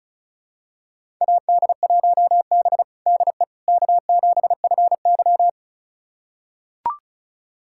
Morse code, heard as a keyed tone, about 36 wpm, 700 Hz. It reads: AD1B DE K7FY
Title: USA Callsign DE Callsign - 4 Characters - 35wpm